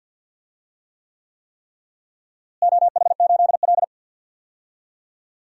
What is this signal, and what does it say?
Morse code, heard as a keyed tone, about 49 wpm, 700 Hz. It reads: OH8L
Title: DXCC-100 Call Signs - 4 Characters - Repeated 3x - 50wpm